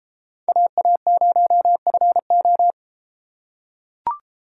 Morse code, heard as a keyed tone, about 33 wpm, 700 Hz. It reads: AA0FO